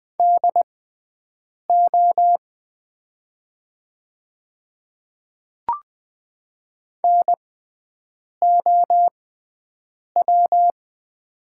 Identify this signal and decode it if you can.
Morse code, heard as a keyed tone, about 20 wpm, spaced as for 7 wpm, 700 Hz. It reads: DO NOW
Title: 6x Character Spacing - Top 100 Words - 20wpm